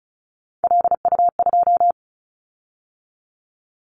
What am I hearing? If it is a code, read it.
Morse code, heard as a keyed tone, about 34 wpm, 700 Hz. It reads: LU2
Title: Callsign Prefixes for Top 50 DXCC - 35wpm